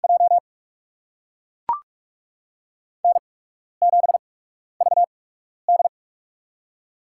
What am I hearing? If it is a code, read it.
Morse code, heard as a keyed tone, about 44 wpm, spaced as for 12 wpm, 700 Hz. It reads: J N7VD
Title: USA Call Signs - 4 Characters - 8x Character Spacing - 45wpm